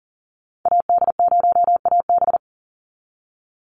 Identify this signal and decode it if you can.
Morse code, heard as a keyed tone, about 40 wpm, 700 Hz. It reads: AD0AB
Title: Call Signs Repeated 3x - 40wpm